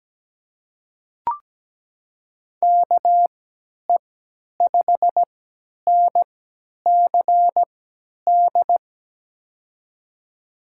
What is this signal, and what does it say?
Morse code, heard as a keyed tone, about 17 wpm, spaced as for 10 wpm, 700 Hz. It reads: KE5NCD
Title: USA Call Signs - 6 Characters - 3x Character Spacing - 17wpm